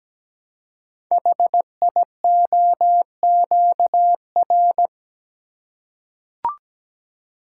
Morse code, heard as a keyed tone, about 17 wpm, 700 Hz. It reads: HIOQR
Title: Code Groups - 5 Letter:Numbers - 17wpm